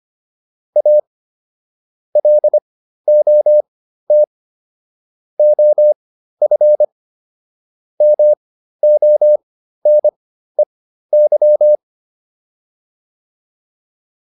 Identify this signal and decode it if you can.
Morse code, heard as a keyed tone, about 25 wpm, spaced as for 13 wpm, 600 Hz. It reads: A LOT OF MONEY